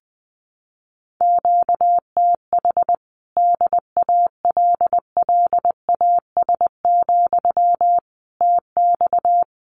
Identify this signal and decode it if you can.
Morse code, heard as a keyed tone, about 20 wpm, 700 Hz. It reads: QTH DALLAS, TX
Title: Common QSO Phrases Encore 20wpm